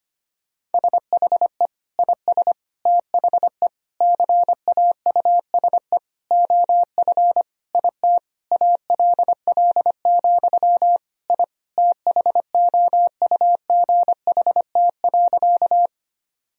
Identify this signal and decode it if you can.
Morse code, heard as a keyed tone, about 25 wpm, 700 Hz. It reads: SHE IS THE CAUSE OF IT ALL, I THOUGHT.
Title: Sentences from Top 1000 Words - Review Part 1 - 25wpm